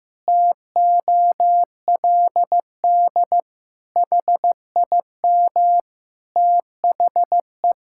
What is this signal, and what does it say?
Morse code, heard as a keyed tone, about 15 wpm, 700 Hz. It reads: TOLD HIM THE
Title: Pangram Challenge v2 - 15wpm